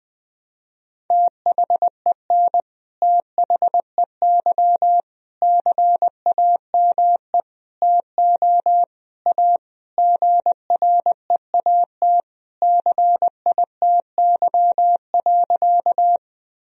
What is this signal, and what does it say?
Morse code, heard as a keyed tone, about 20 wpm, 700 Hz. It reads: THEN THEY CAME TO A GREAT CITY.
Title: Sentences from Top 1000 Words - Review Part 1 - 20wpm